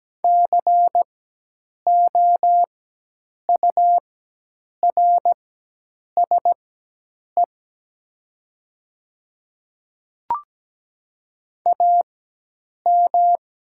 Morse code, heard as a keyed tone, about 17 wpm, spaced as for 8 wpm, 700 Hz. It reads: COURSE AM
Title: 4x Character Spacing - Top 500 Words - 17wpm